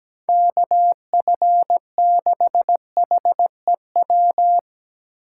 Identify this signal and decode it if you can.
Morse code, heard as a keyed tone, about 17 wpm, 700 Hz. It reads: KF6HEW
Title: USA Call Signs - 4x Character Spacing - 17wpm